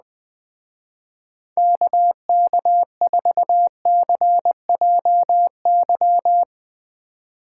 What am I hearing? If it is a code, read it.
Morse code, heard as a keyed tone, about 20 wpm, 700 Hz. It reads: KK4CJY